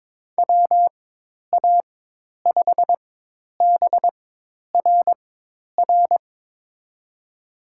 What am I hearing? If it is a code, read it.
Morse code, heard as a keyed tone, about 22 wpm, spaced as for 10 wpm, 700 Hz. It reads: WA5BRR
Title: USA Call Signs - 6 Characters - 4x Character Spacing - 22wpm